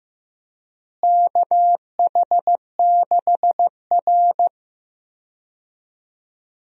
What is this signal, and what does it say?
Morse code, heard as a keyed tone, about 15 wpm, 700 Hz. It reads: KH6R
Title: DXCC-100 Call Signs - 4 Characters - Repeated 2x - 15wpm